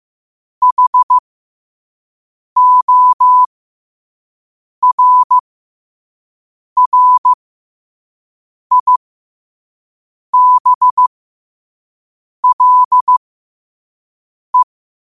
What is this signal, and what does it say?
Morse code, heard as a keyed tone, about 15 wpm, spaced as for 5 wpm, 1000 Hz. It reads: HORRIBLE